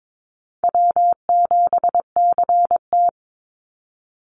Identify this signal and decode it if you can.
Morse code, heard as a keyed tone, about 22 wpm, 700 Hz. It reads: W7CT
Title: Easy Call Signs - 22wpm